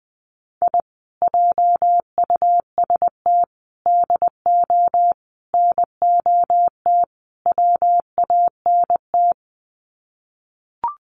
Morse code, heard as a keyed tone, about 20 wpm, 700 Hz. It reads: I JUST DO NOT WANT
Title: Sets of 5 Words Encore2 20wpm